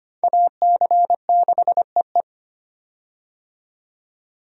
Morse code, heard as a keyed tone, about 25 wpm, 700 Hz. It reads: AC6EE
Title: USA Call Signs - 5 Characters - 25wpm